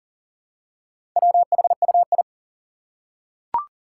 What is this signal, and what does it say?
Morse code, heard as a keyed tone, about 40 wpm, 700 Hz. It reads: WHUI
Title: Four Challenging Letter-Number Combinations 40wpm